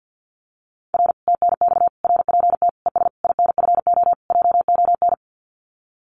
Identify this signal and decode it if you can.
Morse code, heard as a keyed tone, about 50 wpm, 700 Hz. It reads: R TNX RPT ES INFO JON
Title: Common QSO Phrases Encore 50wpm